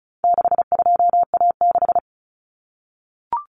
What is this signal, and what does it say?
Morse code, heard as a keyed tone, about 34 wpm, 700 Hz. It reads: 62A6
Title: Common Cut Numbers - 4-Digit 35wpm